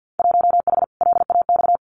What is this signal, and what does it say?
Morse code, heard as a keyed tone, about 50 wpm, 700 Hz. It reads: 1H LAX
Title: Winter Field Day Exchanges 50wpm